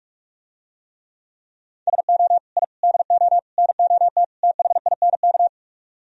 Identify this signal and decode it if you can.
Morse code, heard as a keyed tone, about 46 wpm, 700 Hz. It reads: SO I DO NOT THINK